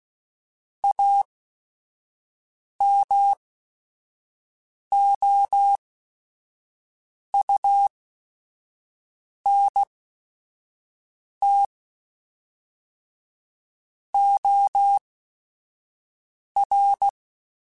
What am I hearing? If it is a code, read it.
Morse code, heard as a keyed tone, about 16 wpm, spaced as for 5 wpm, 800 Hz. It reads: AMOUNT OR
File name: Technical Subjects / Technical Subjects-mp3s / XC @5